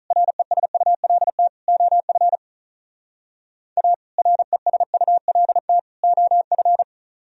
Morse code, heard as a keyed tone, about 44 wpm, 700 Hz. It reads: RESULT OF A RESULT OF